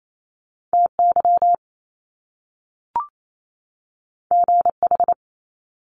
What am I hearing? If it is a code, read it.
Morse code, heard as a keyed tone, about 28 wpm, 700 Hz. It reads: TY GH
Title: Top Two Letter Combinations in Words 28wpm